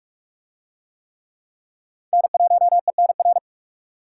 Morse code, heard as a keyed tone, about 44 wpm, 700 Hz. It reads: N1ENR